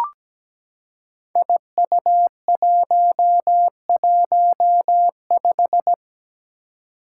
Morse code, heard as a keyed tone, about 17 wpm, 700 Hz. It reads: IU115
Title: Five Challenging Letter-Number Combinations 17wpm